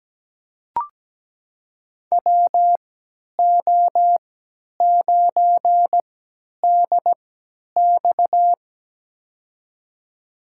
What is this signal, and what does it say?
Morse code, heard as a keyed tone, about 17 wpm, spaced as for 10 wpm, 700 Hz. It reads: WO9DX
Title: USA Call Signs - 5 Characters - 3x Character Spacing - 17wpm